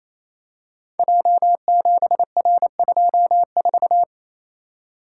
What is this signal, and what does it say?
Morse code, heard as a keyed tone, about 27 wpm, 700 Hz. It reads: J7R24
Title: Code Groups - 5 Letter:Numbers - 28wpm